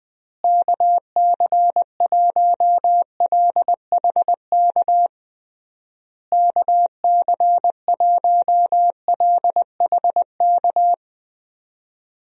Morse code, heard as a keyed tone, about 20 wpm, 700 Hz. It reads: KC1LHK KC1LHK
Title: USA Call Signs - 6 Characters - Repeated 3x - 20wpm